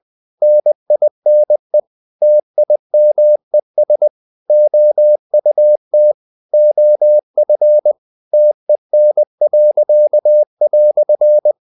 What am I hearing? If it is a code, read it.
Morse code, heard as a keyed tone, about 20 wpm, 600 Hz. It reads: NINE TIMES OUT OF TEN."
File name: CW / Practice / EnglishQuotations0320